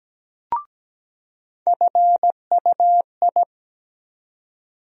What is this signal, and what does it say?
Morse code, heard as a keyed tone, about 17 wpm, 700 Hz. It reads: FUI
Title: USA Call Sign Postfixes - 3 Letters 17wpm